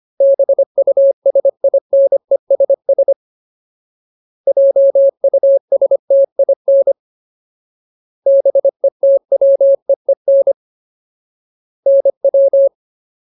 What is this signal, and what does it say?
Morse code, heard as a keyed tone, about 25 wpm, 550 Hz. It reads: BUSINESS JUSTIN BETWEEN NW
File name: Ultimate Word List / 25WPM Ultimate